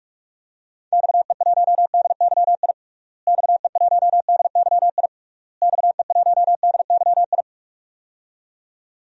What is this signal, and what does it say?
Morse code, heard as a keyed tone, about 46 wpm, 700 Hz. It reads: XE1DYI XE1DYI XE1DYI
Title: DXCC-100 Call Signs - 6 Characters - Repeated 3x - 45wpm